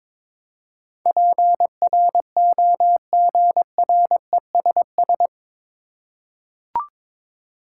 Morse code, heard as a keyed tone, about 22 wpm, 700 Hz. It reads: PROGRESS
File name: practice / 22wpm / Eight Letter Words - 22wpm